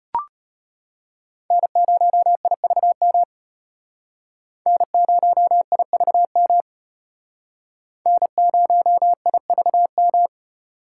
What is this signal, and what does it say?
Morse code, heard as a keyed tone, about 32 wpm, 700 Hz. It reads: N0IVM N0IVM N0IVM